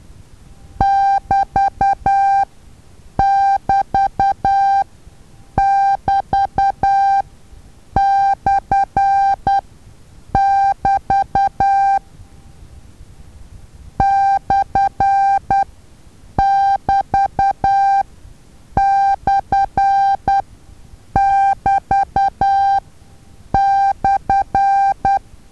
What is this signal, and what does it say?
Morse code, heard as a keyed tone, about 10 wpm, spaced as for 7 wpm, 790 Hz. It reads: <BT><BT><BT>/<BT> /<BT>/<BT>/
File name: Taviro HA6VA A